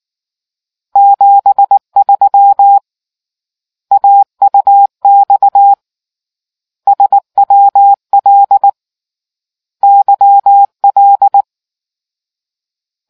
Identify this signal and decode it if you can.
Morse code, heard as a keyed tone, about 19 wpm, 800 Hz. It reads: 73 AUX SWL YL